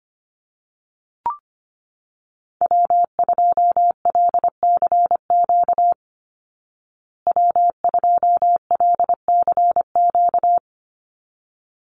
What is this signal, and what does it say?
Morse code, heard as a keyed tone, about 25 wpm, 700 Hz. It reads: W2LCQ W2LCQ
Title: Call Signs Repeated 3x - 25wpm